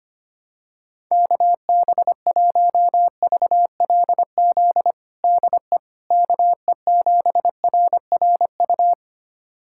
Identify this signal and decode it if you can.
Morse code, heard as a keyed tone, about 25 wpm, 700 Hz. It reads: KB1VLZ DE KE7RRU
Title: USA Callsign DE Callsign 25wpm